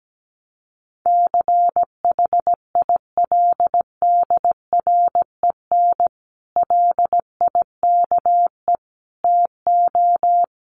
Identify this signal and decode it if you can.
Morse code, heard as a keyed tone, about 17 wpm, 700 Hz. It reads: CHILDREN LIKE TO
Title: Sentences from Top 1300 Words 17wpm